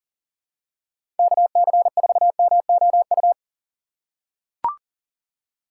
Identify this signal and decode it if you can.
Morse code, heard as a keyed tone, about 40 wpm, 700 Hz. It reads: KC4MOU
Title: USA Call Signs - 6 Characters - 3x Character Spacing - 40wpm